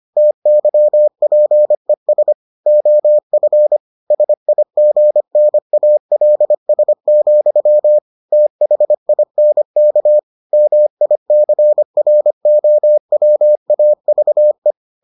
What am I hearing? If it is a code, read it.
Morse code, heard as a keyed tone, about 25 wpm, 600 Hz. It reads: TYPES OF SIGNALS, THINK MICROWAVE